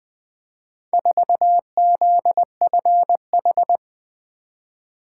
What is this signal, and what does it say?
Morse code, heard as a keyed tone, about 20 wpm, 700 Hz. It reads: 4ZFH
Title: Four Letter-Number Combinations - 20wpm